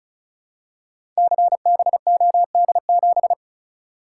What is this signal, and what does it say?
Morse code, heard as a keyed tone, about 36 wpm, 700 Hz. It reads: CBOD7
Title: Code Groups - 5 Letter:Numbers - 35wpm